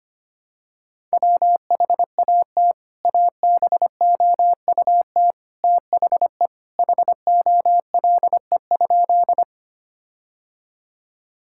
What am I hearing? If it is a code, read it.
Morse code, heard as a keyed tone, about 25 wpm, 700 Hz. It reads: WHAT ABOUT THE HOLE?
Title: Sentences from Top 1600 Words 25wpm